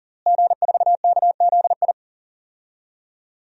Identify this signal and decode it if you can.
Morse code, heard as a keyed tone, about 40 wpm, 700 Hz. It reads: G4KZI
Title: DXCC-100 Call Signs - 5 Characters - Repeated 2x - 40wpm